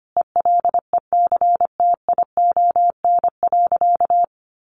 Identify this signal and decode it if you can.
Morse code, heard as a keyed tone, about 25 wpm, 700 Hz. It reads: ELECTION.
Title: Sentences from Top 1500 Words 25wpm